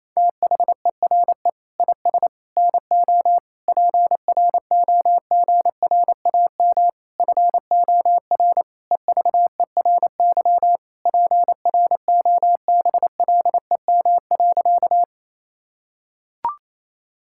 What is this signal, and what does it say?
Morse code, heard as a keyed tone, about 27 wpm, 700 Hz. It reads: THERE IS NO PROGRAM FOR EVERY PROBLEM.
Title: Sentences from Top 1000 Words - Review Part 5 - 28wpm